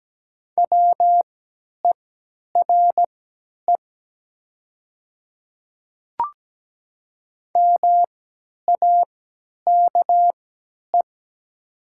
Morse code, heard as a keyed tone, about 17 wpm, spaced as for 10 wpm, 700 Hz. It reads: WERE MAKE
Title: 3x Character Spacing - Top 100 Words - 17wpm